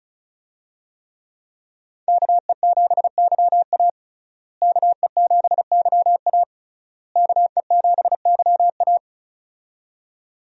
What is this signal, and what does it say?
Morse code, heard as a keyed tone, about 36 wpm, 700 Hz. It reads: KE7YA KE7YA KE7YA
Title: USA Call Signs - 5 Characters - Repeated 3x - 35wpm